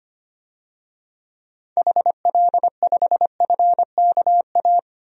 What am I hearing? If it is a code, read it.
Morse code, heard as a keyed tone, about 25 wpm, 700 Hz. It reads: HL5FKA
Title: DXCC-100 Call Signs - 6 Characters - 25wpm